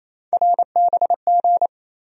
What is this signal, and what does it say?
Morse code, heard as a keyed tone, about 27 wpm, 700 Hz. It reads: RBG
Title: USA Call Sign Postfixes - 3 Letters 28wpm